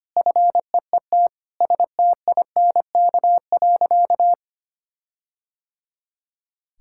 Morse code, heard as a keyed tone, about 25 wpm, 700 Hz. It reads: FEET STINK.
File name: practice / 25wpm / All-Age Appropriate Jokes - 25wpm